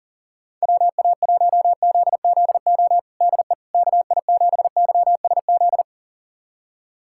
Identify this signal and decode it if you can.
Morse code, heard as a keyed tone, about 40 wpm, 700 Hz. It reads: WA1ZZO DE KI7YSZ
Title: USA Callsign DE Callsign - 6 Characters - 40wpm